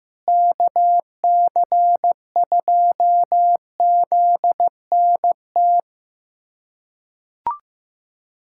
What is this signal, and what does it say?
Morse code, heard as a keyed tone, about 15 wpm, 700 Hz. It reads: KC2ZNT
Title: USA Callsign DE Callsign 15wpm